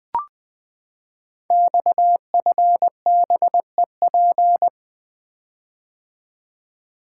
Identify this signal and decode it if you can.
Morse code, heard as a keyed tone, about 20 wpm, 700 Hz. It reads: XFBEP
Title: Code Groups - 5 Letter:Numbers - 20wpm